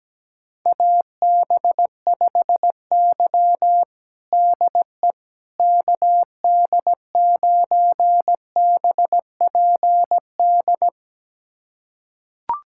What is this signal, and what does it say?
Morse code, heard as a keyed tone, about 17 wpm, 700 Hz. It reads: AB5Y DE KD9BPD
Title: USA Callsign DE Callsign 17wpm